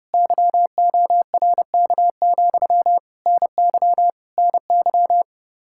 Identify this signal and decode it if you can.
Morse code, heard as a keyed tone, about 33 wpm, 700 Hz. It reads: YORK, NY NY